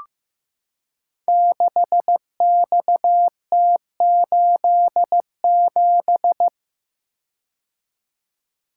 Morse code, heard as a keyed tone, about 15 wpm, 700 Hz. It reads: 6XT87
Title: Code Groups - 5 Letter:Numbers - 15wpm